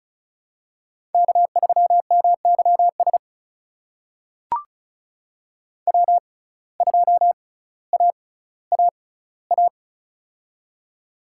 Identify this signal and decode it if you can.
Morse code, heard as a keyed tone, about 35 wpm, spaced as for 18 wpm, 700 Hz. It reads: K3MYS W2AAA
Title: USA Call Signs - 5 Characters - 6x Character Spacing - 35wpm